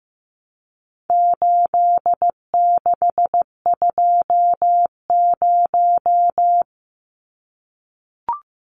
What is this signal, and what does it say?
Morse code, heard as a keyed tone, about 15 wpm, 700 Hz. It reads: 8620
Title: Numbers - 4-Digit 15wpm